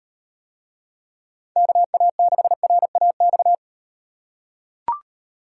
Morse code, heard as a keyed tone, about 36 wpm, 700 Hz. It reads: KA6RAX